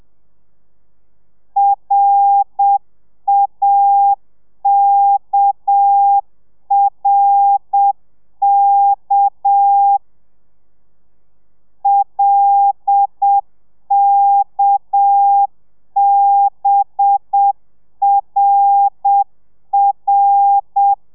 Morse code, heard as a keyed tone, about 7 wpm, 800 Hz. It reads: RAKRK LKBRR